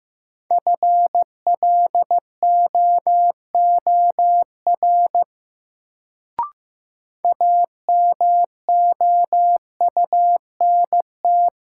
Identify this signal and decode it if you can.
Morse code, heard as a keyed tone, about 15 wpm, 700 Hz. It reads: FLOOR AMOUNT
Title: Sets of 3 Words - New Word Drill - 15wpm